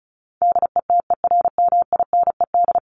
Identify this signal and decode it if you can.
Morse code, heard as a keyed tone, about 34 wpm, 700 Hz. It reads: DETERMINED
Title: ED Suffix - Top 100 Words with Base Words 35wpm